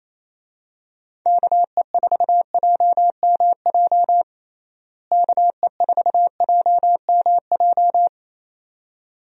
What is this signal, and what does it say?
Morse code, heard as a keyed tone, about 28 wpm, 700 Hz. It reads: KE4JMJ KE4JMJ
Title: USA Call Signs - 6 Characters - Repeated 2x - 28wpm